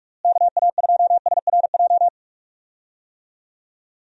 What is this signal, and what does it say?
Morse code, heard as a keyed tone, about 41 wpm, 700 Hz. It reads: KA2SRJ
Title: USA Call Signs - 6 Characters - Repeated 3x - 45wpm